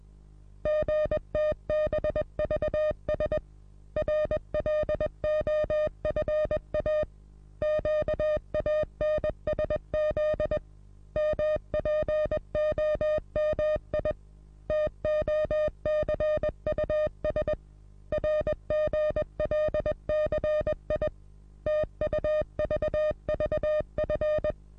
Morse code, heard as a keyed tone, about 21 wpm, 610 Hz. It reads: GTBVS RLOFA QANSZ MPOMI TOCUS RGLCI TUVVF
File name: repos / Morsekurs / DJ4UF Morsen26